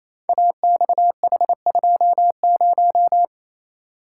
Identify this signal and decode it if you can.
Morse code, heard as a keyed tone, about 27 wpm, 700 Hz. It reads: AXH20
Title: Code Groups - 5 Letter:Numbers - 28wpm